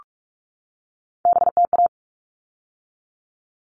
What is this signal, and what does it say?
Morse code, heard as a keyed tone, about 44 wpm, 700 Hz. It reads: BTA
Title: All Cut Numbers - 3-Digit 45wpm